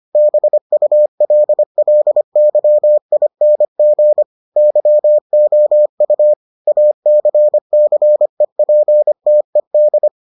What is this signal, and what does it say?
Morse code, heard as a keyed tone, about 25 wpm, 600 Hz. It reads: BULLYING YOU ACCEPTED